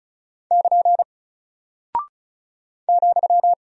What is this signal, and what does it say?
Morse code, heard as a keyed tone, about 34 wpm, 700 Hz. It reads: <KN> ,
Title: Single Character (Letter:Number:Punctuation:Prosign) Alt 35wpm